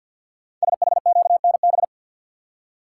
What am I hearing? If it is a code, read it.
Morse code, heard as a keyed tone, about 52 wpm, 700 Hz. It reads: SHQNB
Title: Code Groups - 5 Letters - 50wpm